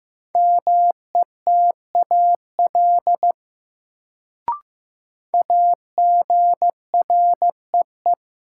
Morse code, heard as a keyed tone, about 15 wpm, 700 Hz. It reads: METAL AGREE
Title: Sentences from Top 1000 Words - Review Part 6 - New Word Drill - 15wpm